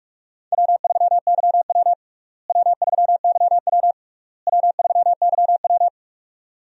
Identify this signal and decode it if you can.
Morse code, heard as a keyed tone, about 42 wpm, 700 Hz. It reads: W3YW W3YW W3YW
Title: USA Call Signs - 4 Characters - Repeated 3x - 45wpm